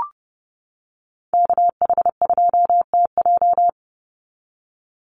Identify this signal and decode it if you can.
Morse code, heard as a keyed tone, about 30 wpm, 700 Hz. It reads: KH2TJ
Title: NA SOTA Activator Callsigns 30wpm